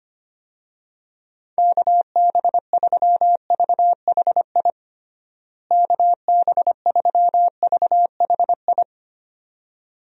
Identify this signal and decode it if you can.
Morse code, heard as a keyed tone, about 25 wpm, 700 Hz. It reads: KB3VHI KB3VHI
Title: USA Call Signs - 6 Characters - Repeated 2x - 25wpm